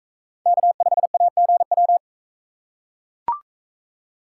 Morse code, heard as a keyed tone, about 44 wpm, 700 Hz. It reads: K5AGW